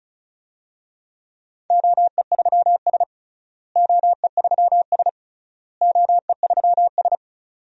Morse code, heard as a keyed tone, about 36 wpm, 700 Hz. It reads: OE3S OE3S OE3S
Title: DXCC-100 Call Signs - 4 Characters - Repeated 3x - 35wpm